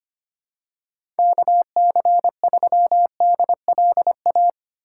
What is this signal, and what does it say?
Morse code, heard as a keyed tone, about 25 wpm, 700 Hz. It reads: KC3DLA
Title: USA Call Signs - 8x Character Spacing - 25wpm